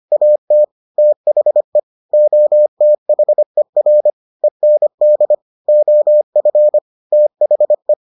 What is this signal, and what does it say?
Morse code, heard as a keyed tone, about 25 wpm, 600 Hz. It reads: AT THE OTHER END OF THE